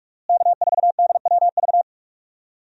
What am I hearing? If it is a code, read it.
Morse code, heard as a keyed tone, about 44 wpm, 700 Hz. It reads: K4DWV
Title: USA Call Signs - 5 Characters - Repeated 2x - 45wpm